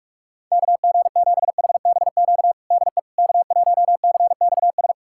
Morse code, heard as a keyed tone, about 45 wpm, 700 Hz. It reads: KG7HBQ DE K1CXS
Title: USA Callsign DE Callsign 45wpm